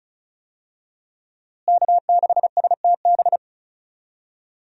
Morse code, heard as a keyed tone, about 36 wpm, 700 Hz. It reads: K6STB
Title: USA Call Signs - 5 Characters - Repeated 3x - 35wpm